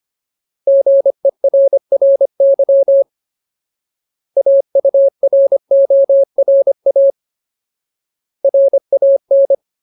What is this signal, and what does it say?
Morse code, heard as a keyed tone, about 25 wpm, 550 Hz. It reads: GERRY AURORA RAN
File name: Ultimate Word List / 25WPM Ultimate